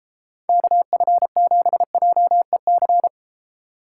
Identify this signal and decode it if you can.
Morse code, heard as a keyed tone, about 34 wpm, 700 Hz. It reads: KF7JEC
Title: Speed Racing - USA 2x3 Call Signs 28wpm